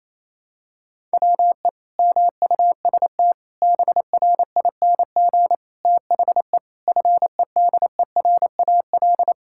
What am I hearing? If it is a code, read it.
Morse code, heard as a keyed tone, about 27 wpm, 700 Hz. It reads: WE MUST BRING THE FEDERAL
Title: Sentences from Top 1000 Words - Review Part 6 - 28wpm